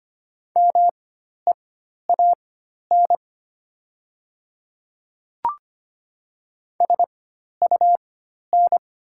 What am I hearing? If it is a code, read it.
Morse code, heard as a keyed tone, about 25 wpm, spaced as for 12 wpm, 700 Hz. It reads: MEAN SUN
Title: 4x Character Spacing - Top 300 Words - 25wpm